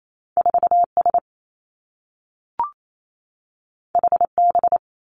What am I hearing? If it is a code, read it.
Morse code, heard as a keyed tone, about 28 wpm, 700 Hz. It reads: 4 S H B
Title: Two Challenging Letter-Number Combinations 28wpm